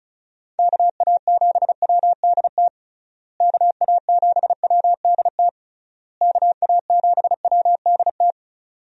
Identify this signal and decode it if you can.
Morse code, heard as a keyed tone, about 36 wpm, 700 Hz. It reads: KA7WDT KA7WDT KA7WDT
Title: USA Call Signs - 6 Characters - Repeated 3x - 35wpm